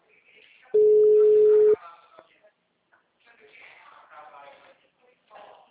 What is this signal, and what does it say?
Ringback tone (Europe, 425 Hz), with speech.